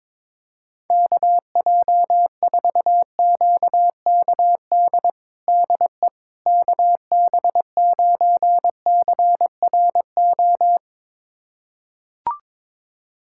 Morse code, heard as a keyed tone, about 22 wpm, 700 Hz. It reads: KJ4QKD DE KB9CRO
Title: USA Callsign DE Callsign - 6 Characters - 22wpm